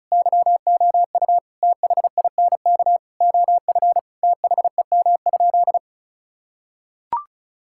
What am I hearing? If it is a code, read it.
Morse code, heard as a keyed tone, about 34 wpm, 700 Hz. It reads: YOU THINK OF THEM?
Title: Sentences from Top 1600 Words 35wpm